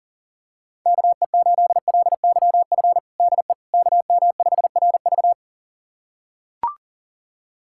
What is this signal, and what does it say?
Morse code, heard as a keyed tone, about 40 wpm, 700 Hz. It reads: KE8LYF DE KM5RV
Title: USA Callsign DE Callsign 40wpm